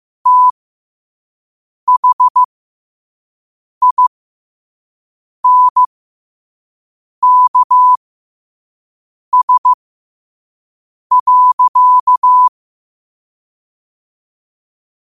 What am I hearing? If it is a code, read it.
Morse code, heard as a keyed tone, about 15 wpm, spaced as for 5 wpm, 1000 Hz. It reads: THINKS.
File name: qotdfiles / 5wpm